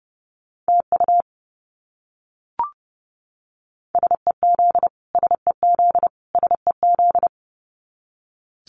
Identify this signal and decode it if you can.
Morse code, heard as a keyed tone, about 30 wpm, 700 Hz. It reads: TU SEZ SEZ SEZ
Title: Abbreviations Repeated 3x - 30wpm